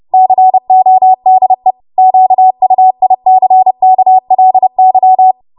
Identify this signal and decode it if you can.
Morse code, heard as a keyed tone, about 30 wpm, 750 Hz. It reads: CODE QUICKLY
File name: Morse / 30WPM